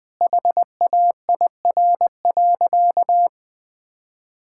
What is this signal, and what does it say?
Morse code, heard as a keyed tone, about 20 wpm, 700 Hz. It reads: HAIR.
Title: Easy Sentences with Hints - 20wpm